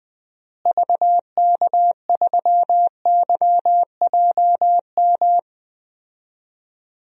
Morse code, heard as a keyed tone, about 20 wpm, 700 Hz. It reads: VK3YJM
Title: DXCC-100 Call Signs - 6 Characters - Repeated 2x - 20wpm